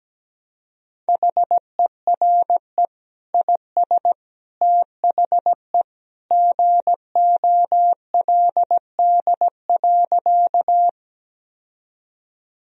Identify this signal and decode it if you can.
Morse code, heard as a keyed tone, about 17 wpm, 700 Hz. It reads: HERE IS THE GOLD.
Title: Sentences from Top 1000 Words - Review Part 3 - 17wpm